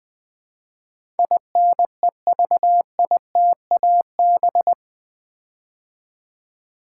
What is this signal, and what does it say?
Morse code, heard as a keyed tone, about 20 wpm, 700 Hz. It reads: INEVITAB
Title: Build A Word - 4 Letters At A Time - 10-Letter Words - 20wpm